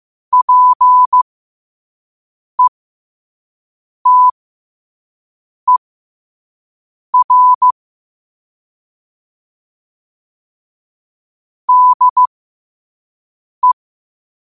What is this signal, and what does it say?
Morse code, heard as a keyed tone, about 15 wpm, spaced as for 5 wpm, 1000 Hz. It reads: PETER DE